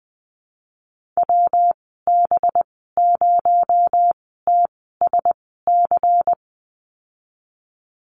Morse code, heard as a keyed tone, about 19 wpm, spaced as for 14 wpm, 700 Hz. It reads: WB0TSC
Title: 219 Lesson 35g - Number 0 - Call Signs 20wpm